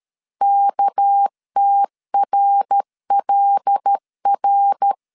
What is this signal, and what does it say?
Morse code, heard as a keyed tone, about 13 wpm, 800 Hz. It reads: KTRLR
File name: CW / DET / 1 9 1500C S B 12wpm 3819